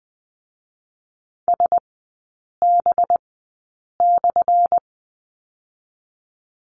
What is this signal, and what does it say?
Morse code, heard as a keyed tone, about 20 wpm, 700 Hz. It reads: S B /
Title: 184 Lesson 30e - Letter b - Three Characters 20wpm